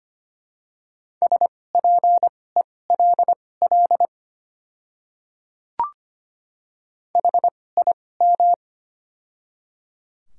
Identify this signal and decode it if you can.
Morse code, heard as a keyed tone, about 25 wpm, spaced as for 18 wpm, 700 Hz. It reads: SPELL HIM
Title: 2x Character Spacing - Top 200 Words - 25wpm